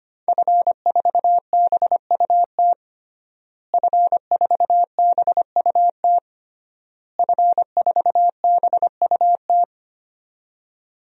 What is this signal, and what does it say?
Morse code, heard as a keyed tone, about 25 wpm, 700 Hz. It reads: F4BUT F4BUT F4BUT
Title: DXCC-100 Call Signs - 5 Characters - Repeated 3x - 25wpm